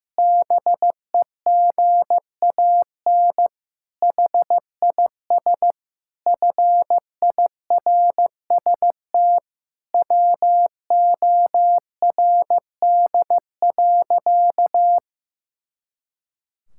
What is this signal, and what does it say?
Morse code, heard as a keyed tone, about 16 wpm, 700 Hz. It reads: BEGAN HIS FIRST WORD.
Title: Pangram Challenge v2 - 15wpm